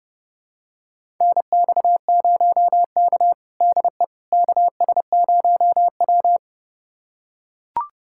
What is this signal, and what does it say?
Morse code, heard as a keyed tone, about 32 wpm, 700 Hz. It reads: NX0K DE KS0W
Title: USA Callsign DE Callsign - 4 Characters - 30wpm